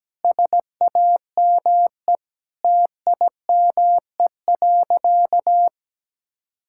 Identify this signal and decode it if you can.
Morse code, heard as a keyed tone, about 17 wpm, 700 Hz. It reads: SAME TIME.
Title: Sentences from Top 1000 Words - Review Part 6 - 17wpm